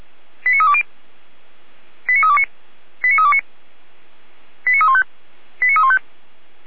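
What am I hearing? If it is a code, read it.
Five-tone selective call (ZVEI), heard as quick runs of five tones, tones of about 70 ms. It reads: 89320 89329 89329 89315 89315
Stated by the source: ZVEI five-tone call received on SDR, from the Feuerwehr Orlamünde audio archive.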